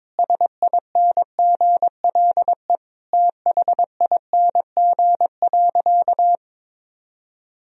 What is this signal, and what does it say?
Morse code, heard as a keyed tone, about 22 wpm, 700 Hz. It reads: SINGLE THING.
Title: Sentences from Top 1000 Words - Review Part 4 - 22wpm